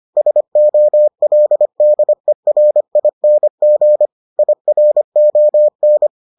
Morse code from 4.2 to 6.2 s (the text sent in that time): IRON